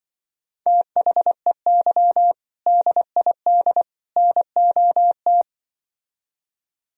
Morse code, heard as a keyed tone, about 24 wpm, 700 Hz. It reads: THEY DID NOT
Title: Speed-Racing - Sets of 3 Words 20wpm